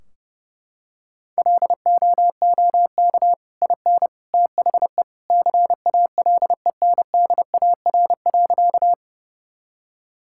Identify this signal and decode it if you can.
Morse code, heard as a keyed tone, about 30 wpm, 700 Hz. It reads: LOOK IN THE CALENDAR.
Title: Sentences from Top 1600 Words 30wpm